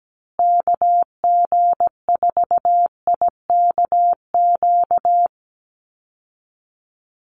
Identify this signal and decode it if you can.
Morse code, heard as a keyed tone, about 17 wpm, 700 Hz. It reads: KG4IKQ
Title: Call Signs Repeated 2x - 17wpm